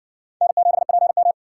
Morse code, heard as a keyed tone, about 54 wpm, 700 Hz. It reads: N7PM